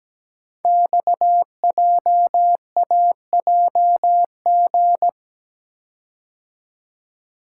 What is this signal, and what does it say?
Morse code, heard as a keyed tone, about 17 wpm, 700 Hz. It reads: XJAJG
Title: Code Groups - 5 Letters - 17wpm